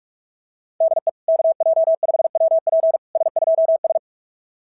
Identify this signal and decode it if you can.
Morse code, heard as a keyed tone, about 44 wpm, 660 Hz. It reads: DE KJ5WP S2S
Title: SOTA QSO Practice 45wpm